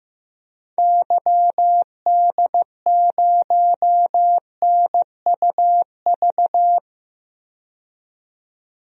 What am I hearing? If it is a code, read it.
Morse code, heard as a keyed tone, about 15 wpm, 700 Hz. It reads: YD0NUV
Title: DXCC-100 Call Signs - 6 Characters - 15wpm